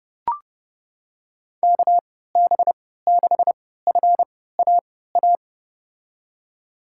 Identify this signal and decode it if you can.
Morse code, heard as a keyed tone, about 30 wpm, spaced as for 17 wpm, 700 Hz. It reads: KB6FAA
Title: USA Call Signs - 3x Character Spacing - 30wpm